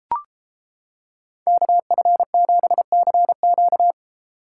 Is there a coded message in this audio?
Yes: Morse code from 1.5 to 3.9 s, heard as a keyed tone, about 29 wpm, 700 Hz. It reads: KF7CQ